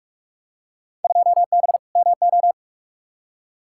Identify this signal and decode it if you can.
Morse code, heard as a keyed tone, about 43 wpm, 700 Hz. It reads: 2B MO
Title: Field Day Exchanges - 45wpm